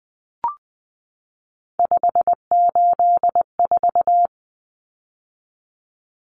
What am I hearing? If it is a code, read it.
Morse code, heard as a keyed tone, about 20 wpm, 700 Hz. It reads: 584
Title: Common Cut Numbers - 3-Digit 20wpm